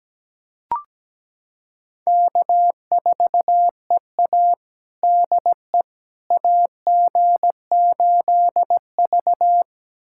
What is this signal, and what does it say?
Morse code, heard as a keyed tone, about 17 wpm, 700 Hz. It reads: K4EA DE AG8V
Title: USA Callsign DE Callsign - 4 Characters - 17wpm